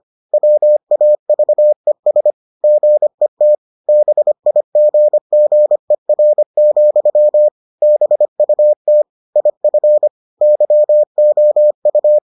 Morse code, heard as a keyed tone, about 25 wpm, 600 Hz. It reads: WAVES GET BIGGER, BUT IF YOU